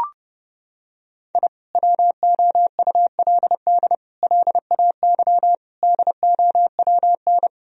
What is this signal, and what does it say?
Morse code, heard as a keyed tone, about 30 wpm, 700 Hz. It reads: I WOULD LAY DOWN
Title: Sentences from Top 1000 Words - Review Part 2 - 30wpm